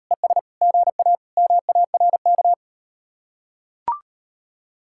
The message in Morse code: ES GA MARK

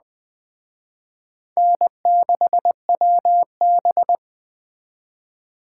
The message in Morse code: N6WB

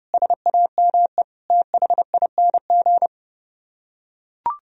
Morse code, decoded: SAME THING

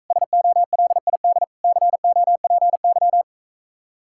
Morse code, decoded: SOLID COPY

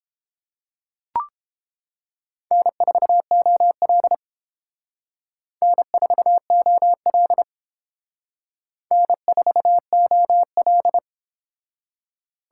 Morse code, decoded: N4OL N4OL N4OL

